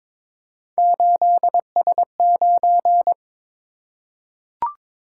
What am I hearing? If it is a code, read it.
Morse code, heard as a keyed tone, about 22 wpm, 700 Hz. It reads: 8S9
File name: practice / 22wpm / Three Letter-Number Combinations - 22wpm